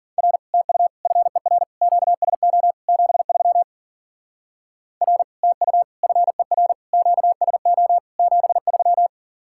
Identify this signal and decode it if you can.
Morse code, heard as a keyed tone, about 48 wpm, 700 Hz. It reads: R TU FER QSO 73 R TU FER QSO 73
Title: Speed-Racing Common QSO Phrases - 40wpm